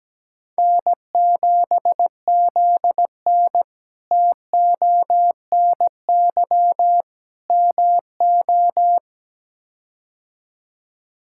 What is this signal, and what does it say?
Morse code, heard as a keyed tone, about 17 wpm, 700 Hz. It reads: N7ZN TONY MO
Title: K1USN SST QSO Practice - 17wpm